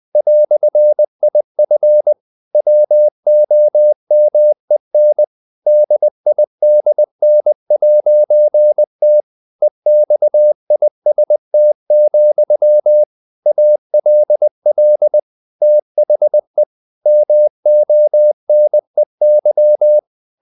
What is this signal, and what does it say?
Morse code, heard as a keyed tone, about 20 wpm, 600 Hz. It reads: "IF WOMEN DIDN'T EXIST, ALL THE MONEY
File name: CW / Practice / EnglishQuotations0320